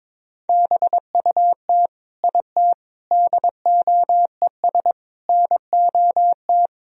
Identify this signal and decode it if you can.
Morse code, heard as a keyed tone, about 22 wpm, 700 Hz. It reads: BUT IT DOES NOT
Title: Sentences from Top 1000 Words - Review Part 2 - 22wpm